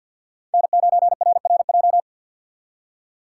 Morse code, heard as a keyed tone, about 50 wpm, 700 Hz. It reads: N9RRJ
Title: USA Call Signs - 5 Characters - 4x Character Spacing - 50wpm